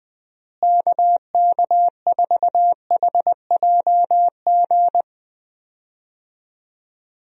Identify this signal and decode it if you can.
Morse code, heard as a keyed tone, about 21 wpm, 700 Hz. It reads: KK4HJG
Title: USA Call Signs - 6 Characters - Repeated 3x - 20wpm